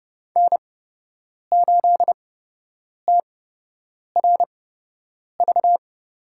Morse code, heard as a keyed tone, about 31 wpm, spaced as for 8 wpm, 700 Hz. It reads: N8TRV
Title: USA Call Signs - 8x Character Spacing - 30wpm